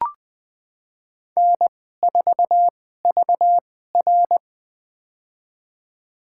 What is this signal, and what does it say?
Morse code, heard as a keyed tone, about 20 wpm, spaced as for 14 wpm, 700 Hz. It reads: N4VR
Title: USA Call Signs - 4 Characters - 2x Character Spacing - 20wpm